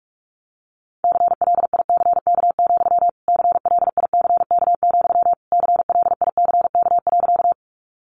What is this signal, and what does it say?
Morse code, heard as a keyed tone, about 44 wpm, 700 Hz. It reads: CLICK, CLICK, CLICK.
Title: Sentences from Top 1100 Words 45wpm